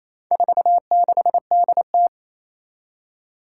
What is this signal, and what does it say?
Morse code, heard as a keyed tone, about 28 wpm, 700 Hz. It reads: 46DT
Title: Four Challenging Letter-Number Combinations 28wpm